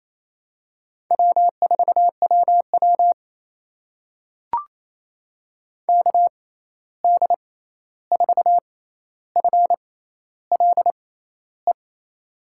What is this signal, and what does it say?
Morse code, heard as a keyed tone, about 28 wpm, spaced as for 13 wpm, 700 Hz. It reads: W4WW KD4FLE
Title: USA Call Signs - 6x Character Spacing - 28wpm